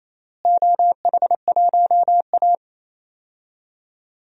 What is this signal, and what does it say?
Morse code, heard as a keyed tone, about 28 wpm, 700 Hz. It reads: OH1A
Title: DXCC-100 Call Signs - 4 Characters - Repeated 3x - 28wpm